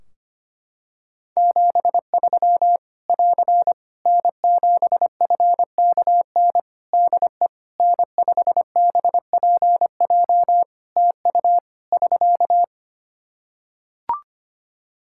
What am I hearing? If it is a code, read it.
Morse code, heard as a keyed tone, about 25 wpm, 700 Hz. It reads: 73 <AR> N7FKN DE N5BPJ TU <SK>
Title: Practice QSO - Exchange 4 - Ending - 25wpm